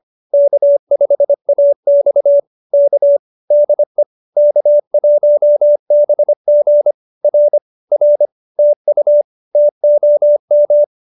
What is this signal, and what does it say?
Morse code, heard as a keyed tone, about 25 wpm, 590 Hz. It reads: K5AX K DE K1BG R R TU TOM